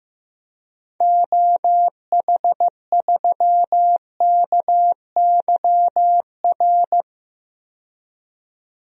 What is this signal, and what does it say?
Morse code, heard as a keyed tone, about 15 wpm, 700 Hz. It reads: OH3KYR
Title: DXCC-100 Call Signs - 6 Characters - 15wpm